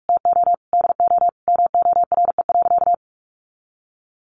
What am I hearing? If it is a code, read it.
Morse code, heard as a keyed tone, about 44 wpm, 700 Hz. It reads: TO DO MORE.
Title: Sentences from Top 1400 Words 45wpm